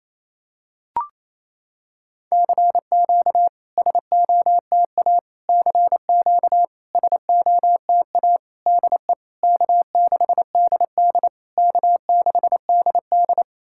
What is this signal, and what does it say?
Morse code, heard as a keyed tone, about 27 wpm, 690 Hz. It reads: CQ SOTA CQ SOTA DE K6DD K6DD
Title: SOTA QSO Practice 28wpm